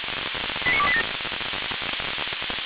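Five-tone selective call (ZVEI), heard as quick runs of five tones, tones of about 70 ms. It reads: 89287
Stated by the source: ZVEI five-tone selective call 89287, received via SDR, recorded by Feuerwehr Orlamünde fire brigade.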